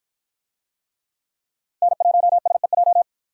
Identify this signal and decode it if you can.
Morse code, heard as a keyed tone, about 49 wpm, 700 Hz. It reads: N1SEJ